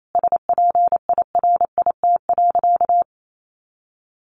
Morse code, heard as a keyed tone, about 28 wpm, 700 Hz. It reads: SPIRIT.
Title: Sentences from Top 1400 Words 28wpm